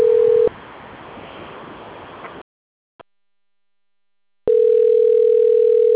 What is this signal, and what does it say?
Ringback tone (North America), 440 + 480 Hz.